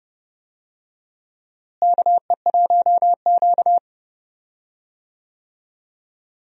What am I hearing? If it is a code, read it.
Morse code, heard as a keyed tone, about 29 wpm, 700 Hz. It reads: KE1Q
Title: USA Call Signs - 4 Characters - Repeated 2x - 30wpm